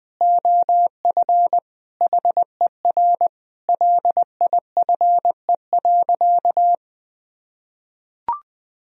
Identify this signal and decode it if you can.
Morse code, heard as a keyed tone, about 21 wpm, 700 Hz. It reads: OF HER LIFE.
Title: Sentences from Top 1000 Words - Review Part 3 - 20wpm